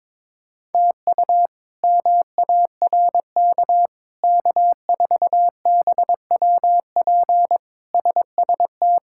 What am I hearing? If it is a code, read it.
Morse code, heard as a keyed tone, about 22 wpm, 700 Hz. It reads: TU MARK K4BWP SST
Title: K1USN SST QSO Practice - 22wpm